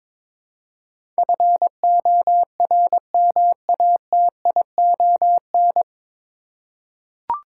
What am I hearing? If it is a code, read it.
Morse code, heard as a keyed tone, about 22 wpm, 700 Hz. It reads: FORMATION
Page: Morse Code Ninja practice 22wpm